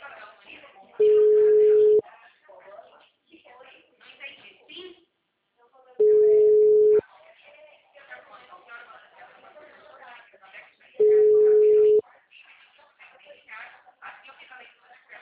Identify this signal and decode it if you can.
Ringback tone (Europe, 425 Hz), with speech.